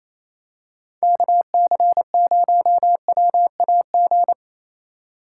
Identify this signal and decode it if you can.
Morse code, heard as a keyed tone, about 28 wpm, 700 Hz. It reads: KC0WAG
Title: USA Call Signs - 6 Characters - 3x Character Spacing - 28wpm